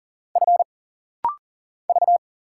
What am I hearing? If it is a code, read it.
Morse code, heard as a keyed tone, about 41 wpm, 700 Hz. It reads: F V
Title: Single Letter-Number Alt - Rapid-Fire 40wpm